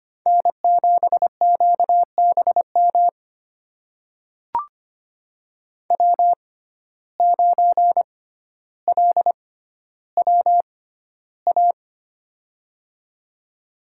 Morse code, heard as a keyed tone, about 25 wpm, spaced as for 13 wpm, 700 Hz. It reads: N7QBM W9LWA
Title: USA Call Signs - 5 Characters - 6x Character Spacing - 25wpm